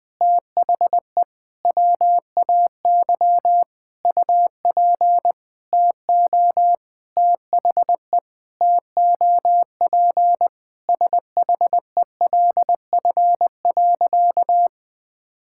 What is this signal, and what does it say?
Morse code, heard as a keyed tone, about 19 wpm, 700 Hz. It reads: THE WAY UP TO THE TOP SHELF.
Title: Sentences with 6-Word Phrases - Spoken Beginning - 20wpm